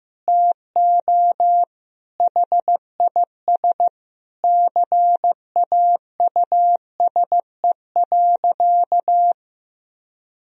Morse code, heard as a keyed tone, about 15 wpm, 700 Hz. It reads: TO HIS CAUSE.
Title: Sentences from Top 1000 Words - Review Part 4 - 15wpm